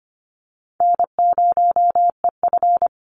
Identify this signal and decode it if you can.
Morse code, heard as a keyed tone, about 25 wpm, 700 Hz. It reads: N0EF
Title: Easy Call Signs - 25wpm